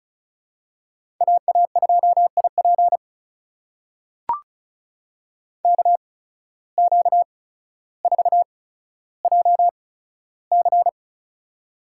Morse code, heard as a keyed tone, about 36 wpm, spaced as for 15 wpm, 700 Hz. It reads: AA2IP KQ4JC
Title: USA Call Signs - 5 Characters - 8x Character Spacing - 35wpm